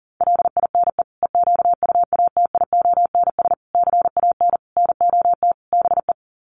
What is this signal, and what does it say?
Morse code, heard as a keyed tone, about 40 wpm, 700 Hz. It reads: LINE EQUATIONS CAN NOT BE
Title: Sets of 5 Words 40wpm